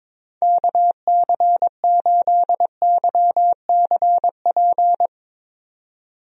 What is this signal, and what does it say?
Morse code, heard as a keyed tone, about 22 wpm, 700 Hz. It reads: KC8YCP
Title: USA Call Signs - 6 Characters - 22wpm